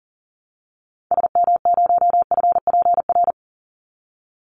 Morse code, heard as a keyed tone, about 40 wpm, 700 Hz. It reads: SM0FPR
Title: Call Signs Repeated 3x - 40wpm